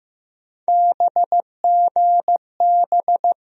Morse code, heard as a keyed tone, about 15 wpm, 700 Hz. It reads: BGB
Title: USA Call Sign Postfixes - 3 Letters 15wpm